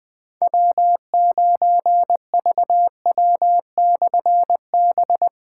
Morse code, heard as a keyed tone, about 20 wpm, 700 Hz. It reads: W9VW/B
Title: Difficult Call Signs 20wpm